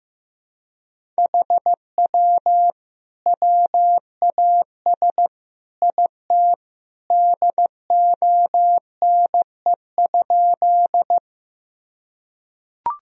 Morse code, heard as a keyed tone, about 14 wpm, 700 Hz. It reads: HW WAS IT DONE?
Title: Sentences from Top 500 Words and Abbreviations 15wpm